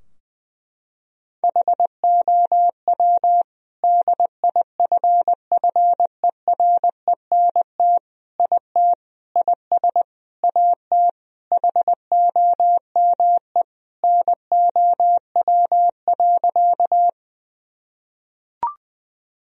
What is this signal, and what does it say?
Morse code, heard as a keyed tone, about 20 wpm, 700 Hz. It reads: HOW DIFFERENT IT IS AT HOME NOW.
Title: Sentences from Top 1000 Words - Review Part 3 - 20wpm